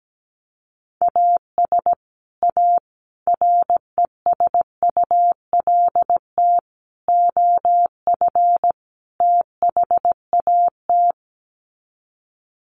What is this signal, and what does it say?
Morse code, heard as a keyed tone, about 17 wpm, 700 Hz. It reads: AS A RESULT OF THAT
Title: Sets of 5 Words Encore 17wpm